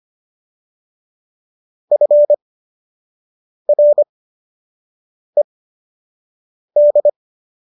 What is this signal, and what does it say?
Morse code, heard as a keyed tone, about 25 wpm, spaced as for 6 wpm, 590 Hz. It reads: FRED